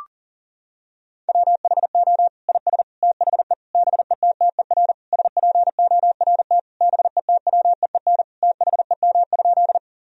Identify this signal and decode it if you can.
Morse code, heard as a keyed tone, about 40 wpm, 700 Hz. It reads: WHO IS THE BETTER SPORT BETWEEN THEM?
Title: Sentences from Top 1000 Words - Review Part 4 - 40wpm